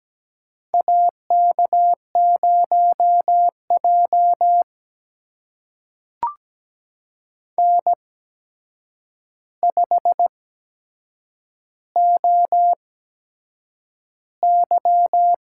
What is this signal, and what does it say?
Morse code, heard as a keyed tone, about 17 wpm, spaced as for 7 wpm, 700 Hz. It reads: AK0J N5OY
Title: USA Call Signs - 4 Characters - 8x Character Spacing - 17wpm